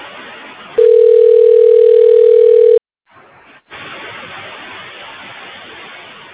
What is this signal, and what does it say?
Ringback tone (North America), 440 + 480 Hz.